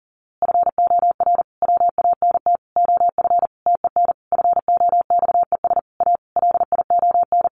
Morse code, heard as a keyed tone, about 40 wpm, 700 Hz. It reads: FOR WANT OF TEN FOXES A LION
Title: A Sound Of Thunder 40wpm